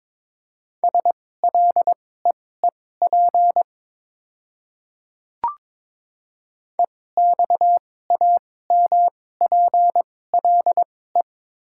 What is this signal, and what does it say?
Morse code, heard as a keyed tone, about 22 wpm, spaced as for 16 wpm, 700 Hz. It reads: SLEEP EXAMPLE
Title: 2x Character Spacing - Top 500 Words - 22wpm